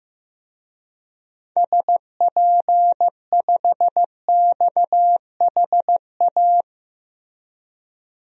Morse code, heard as a keyed tone, about 15 wpm, 700 Hz. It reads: SP5XHA
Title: DXCC-100 Call Signs - 6 Characters - Repeated 2x - 15wpm